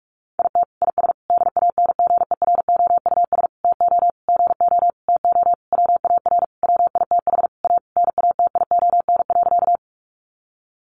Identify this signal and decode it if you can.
Morse code, heard as a keyed tone, about 44 wpm, 700 Hz. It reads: IT IS DANGEROUS TO GO TO WAR WITH A NATION.
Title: Sentences from Top 1200 Words 45wpm